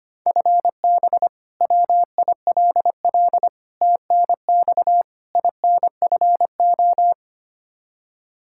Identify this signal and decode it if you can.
Morse code, heard as a keyed tone, about 25 wpm, 700 Hz. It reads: FB WILL TNX INFO
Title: Practice QSO - Exchange 3 - More Info - 25wpm